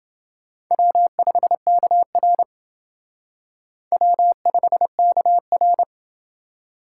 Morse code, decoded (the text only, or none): W5KR W5KR